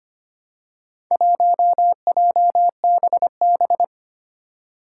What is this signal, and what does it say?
Morse code, heard as a keyed tone, about 25 wpm, 700 Hz. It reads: 1JBB
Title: Four Challenging Letter-Number Combinations 25wpm